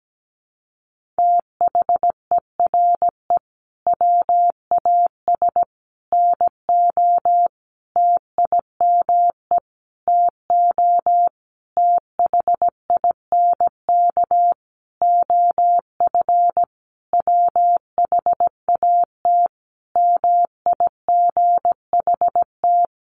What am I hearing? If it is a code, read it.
Morse code, heard as a keyed tone, about 17 wpm, 700 Hz. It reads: THERE WAS NO TIME TO THINK OF WHAT MIGHT
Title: Sentences from Top 400 Words 17wpm